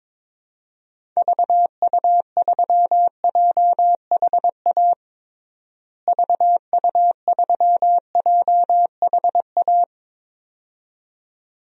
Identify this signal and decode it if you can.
Morse code, heard as a keyed tone, about 22 wpm, 700 Hz. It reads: VU3JHA VU3JHA
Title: DXCC-100 Call Signs - 6 Characters - Repeated 2x - 22wpm